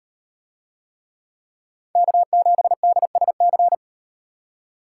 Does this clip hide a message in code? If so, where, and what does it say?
Yes: Morse code from 2.0 to 3.7 s, heard as a keyed tone, about 38 wpm, 700 Hz. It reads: K7DSC